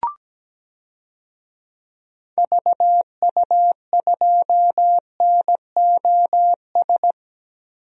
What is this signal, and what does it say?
Morse code, heard as a keyed tone, about 17 wpm, 700 Hz. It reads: VU2NOS
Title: DXCC-100 Call Signs - 6 Characters - Repeated 2x - 17wpm